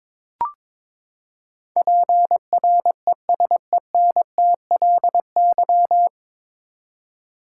Morse code, heard as a keyed tone, about 21 wpm, 700 Hz. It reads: PRESENTLY
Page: Morse Code Ninja practice 22wpm